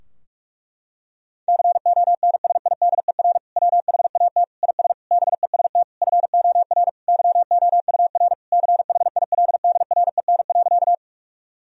Morse code, heard as a keyed tone, about 46 wpm, 700 Hz. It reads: CONSIDER WHAT IS BEST FOR YOUR CHILDREN.
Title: Sentences from Top 1000 Words - Review Part 4 - 45wpm